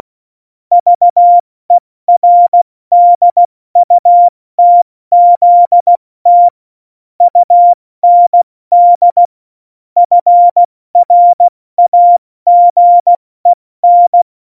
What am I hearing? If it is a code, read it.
Morse code, heard as a keyed tone, about 16 wpm, 700 Hz. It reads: VERDUTZT UND FRAGEN
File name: Morsekurs / Witze / Witz 08. Oktober 2019, 70 F80